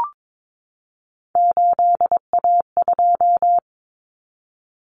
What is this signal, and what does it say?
Morse code, heard as a keyed tone, about 22 wpm, 700 Hz. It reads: 8A2
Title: Common Cut Numbers - 3-Digit 22wpm